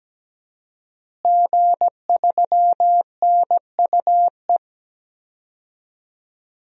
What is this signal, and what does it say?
Morse code, heard as a keyed tone, about 17 wpm, 700 Hz. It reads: G3NUE
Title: DXCC-100 Call Signs - 5 Characters - Repeated 3x - 17wpm